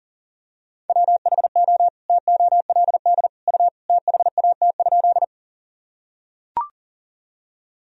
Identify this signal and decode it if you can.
Morse code, heard as a keyed tone, about 40 wpm, 700 Hz. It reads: WHO TOLD U THAT?
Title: Sentences from Top 400 Words and Abbreviations 40wpm